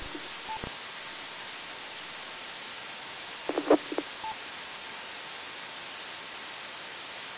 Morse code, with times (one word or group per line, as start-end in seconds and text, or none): none